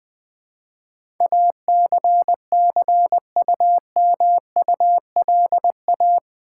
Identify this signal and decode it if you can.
Morse code, heard as a keyed tone, about 20 wpm, 700 Hz. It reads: ACCUMULA
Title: Build A Word - 4 Letters At A Time - 12-Letter Words - 20wpm